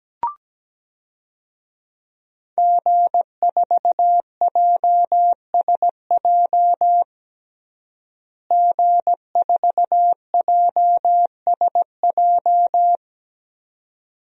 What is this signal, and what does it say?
Morse code, heard as a keyed tone, about 17 wpm, 700 Hz. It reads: G4JSJ G4JSJ
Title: DXCC-100 Call Signs - 5 Characters - Repeated 2x - 17wpm